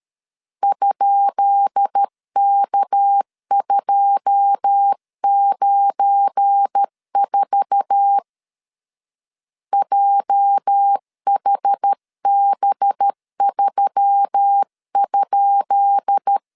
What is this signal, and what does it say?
Morse code, heard as a keyed tone, about 13 wpm, 800 Hz. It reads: ?K294 JHB3?
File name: LEZIONI / 10 SIGNS / 10B INT 12wpm 2414